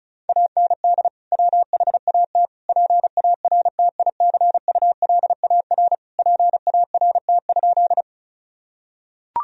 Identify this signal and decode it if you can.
Morse code, heard as a keyed tone, about 34 wpm, 700 Hz. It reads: AND WHAT PARTICULAR PART?
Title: Sentences from Top 1000 Words - Review Part 4 - 35wpm